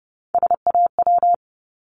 Morse code, heard as a keyed tone, about 30 wpm, 700 Hz. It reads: SAW
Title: Multi-letter Phonemes 30wpm